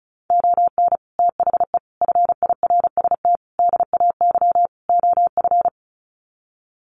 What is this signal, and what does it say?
Morse code, heard as a keyed tone, about 34 wpm, 700 Hz. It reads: ON THE FIRST DAY OF
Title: Sets of 5 Words Encore 35wpm